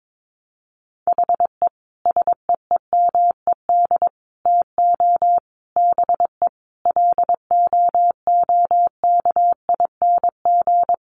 Morse code, heard as a keyed tone, about 22 wpm, 700 Hz. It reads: HE SEEMED TO BE LOOKING
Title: A Sound Of Thunder 22wpm